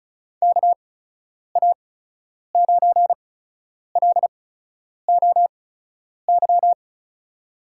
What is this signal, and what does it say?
Morse code, heard as a keyed tone, about 36 wpm, spaced as for 10 wpm, 700 Hz. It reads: KA9LOY
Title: USA Call Signs - 6 Characters - 8x Character Spacing - 35wpm